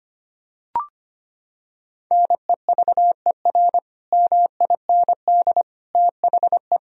Morse code, heard as a keyed tone, about 25 wpm, 700 Hz. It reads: NEVER MIND THE
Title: Sentences from Top 1000 Words - Review Part 4 - 25wpm